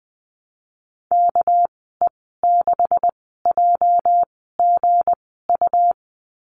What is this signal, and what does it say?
Morse code, heard as a keyed tone, about 20 wpm, spaced as for 14 wpm, 700 Hz. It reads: KE6JGU
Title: 213 Lesson 34g - Number 8 - Call Signs 20wpm